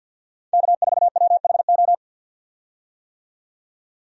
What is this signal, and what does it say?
Morse code, heard as a keyed tone, about 51 wpm, 700 Hz. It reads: K4WHO
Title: USA Call Signs - 5 Characters - 50wpm